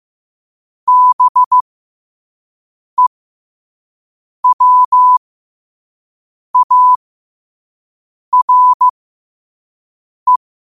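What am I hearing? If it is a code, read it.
Morse code, heard as a keyed tone, about 15 wpm, spaced as for 5 wpm, 1000 Hz. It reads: BEWARE
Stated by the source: Morse code sent at 5 wpm.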